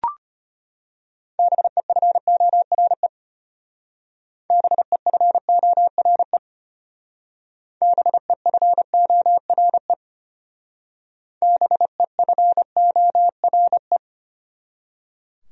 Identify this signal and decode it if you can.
Morse code, heard as a keyed tone, about 29 wpm, 700 Hz. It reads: BEFORE BEFORE BEFORE BEFORE
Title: Speed-Racing Top 300 Words 25wpm